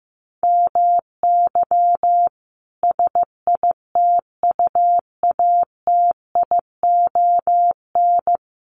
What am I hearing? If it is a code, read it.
Morse code, heard as a keyed tone, about 15 wpm, 700 Hz. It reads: MY SITUATION
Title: Sentences from Top 1200 Words 15wpm